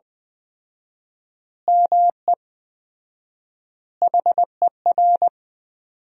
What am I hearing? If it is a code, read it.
Morse code, heard as a keyed tone, about 20 wpm, 700 Hz. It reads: ME HER
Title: 4x Word-Spacing - Sentences from Top 500 Words - 20wpm